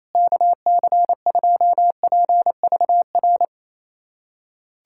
Morse code, heard as a keyed tone, about 28 wpm, 700 Hz. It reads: KC2PVR
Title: USA Callsign DE Callsign - 6 Characters - 28wpm